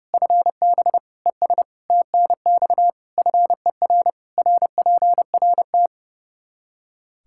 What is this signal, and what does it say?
Morse code, heard as a keyed tone, about 30 wpm, 700 Hz. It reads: FB ES TNX FER RPRT
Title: Practice QSO - Exchange 2 - Info - 30wpm